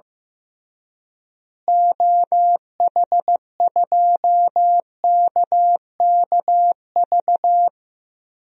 Morse code, heard as a keyed tone, about 15 wpm, 700 Hz. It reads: OH2KKV
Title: DXCC-100 Call Signs - 6 Characters - Repeated 3x - 15wpm